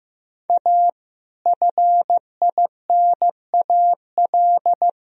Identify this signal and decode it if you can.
Morse code, heard as a keyed tone, about 15 wpm, 700 Hz. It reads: A FINAL
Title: Sentences from Top 1000 Words - Review Part 4 - 15wpm